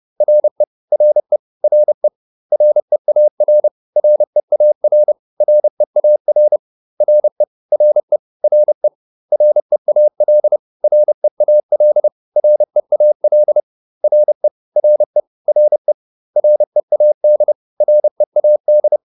Morse code, heard as a keyed tone, about 30 wpm, 600 Hz. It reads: RE RE RE REAR REAR REAR RE RE RE REAL REAL REAL RE RE RE READ READ